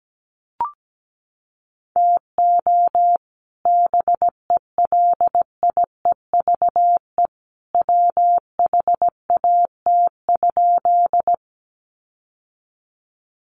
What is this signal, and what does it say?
Morse code, heard as a keyed tone, about 17 wpm, 700 Hz. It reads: TO BELIEVE WHAT?
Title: Sentences from Top 1000 Words 17wpm